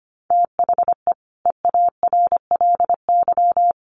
Morse code, heard as a keyed tone, about 25 wpm, 700 Hz. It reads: THE EARLY
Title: Sets of 5 Words Encore2 25wpm